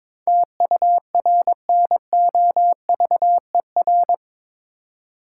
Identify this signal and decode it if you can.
Morse code, heard as a keyed tone, about 22 wpm, 700 Hz. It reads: TURNOVER
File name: practice / 22wpm / Eight Letter Words - 22wpm